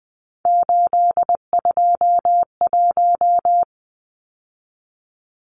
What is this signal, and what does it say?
Morse code, heard as a keyed tone, about 20 wpm, 700 Hz. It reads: 821
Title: Numbers - 3-Digit 20wpm